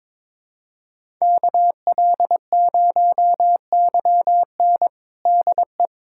Morse code, heard as a keyed tone, about 22 wpm, 700 Hz. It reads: KL0YN DE
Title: USA Callsign DE Callsign - 5 Characters - 22wpm